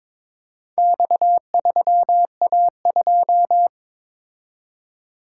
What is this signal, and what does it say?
Morse code, heard as a keyed tone, about 22 wpm, 700 Hz. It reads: X3A2
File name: practice / 22wpm / Four Letter-Number Combinations - 22wpm